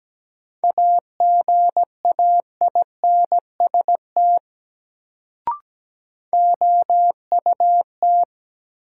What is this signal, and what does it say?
Morse code, heard as a keyed tone, about 17 wpm, 700 Hz. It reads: AGAINST OUT
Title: Sets of 2 Words - New Word Drill - 17wpm